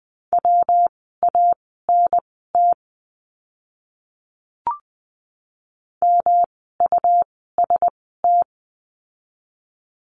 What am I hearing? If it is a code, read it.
Morse code, heard as a keyed tone, about 20 wpm, spaced as for 15 wpm, 700 Hz. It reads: WANT MUST
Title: 237 Lesson 38f - Letter q - Words 20wpm